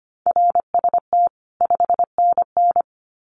Morse code, heard as a keyed tone, about 25 wpm, 700 Hz. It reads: RST 5NN
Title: Common QSO Phrases 25wpm